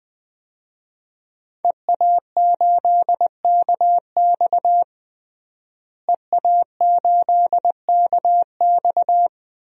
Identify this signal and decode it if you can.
Morse code, heard as a keyed tone, about 20 wpm, 700 Hz. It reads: EA8KX EA8KX
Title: DXCC-100 Call Signs - 5 Characters - Repeated 3x - 20wpm